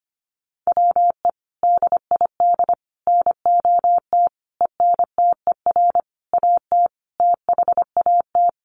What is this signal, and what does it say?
Morse code, heard as a keyed tone, about 25 wpm, 700 Hz. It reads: WE DID NOT ENTER AT THAT
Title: Sentences from Top 1300 Words 25wpm